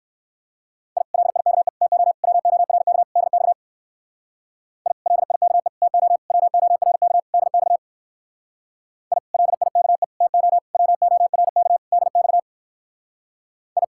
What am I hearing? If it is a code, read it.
Morse code, heard as a keyed tone, about 56 wpm, 700 Hz. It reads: I LIKE TO WORK DX I LIKE TO WORK DX I LIKE TO WORK DX I
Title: Speed-Racing Common QSO Phrases - 45wpm